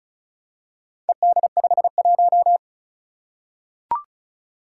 Morse code, heard as a keyed tone, about 34 wpm, 700 Hz. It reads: ED51
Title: Four Challenging Letter-Number Combinations 35wpm